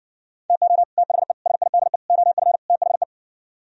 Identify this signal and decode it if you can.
Morse code, heard as a keyed tone, about 59 wpm, 700 Hz. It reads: TO THE SIDE OF THE